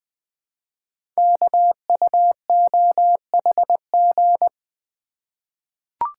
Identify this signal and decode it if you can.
Morse code, heard as a keyed tone, about 20 wpm, 700 Hz. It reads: KUOHG
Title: Code Groups - 5 Letters - 20wpm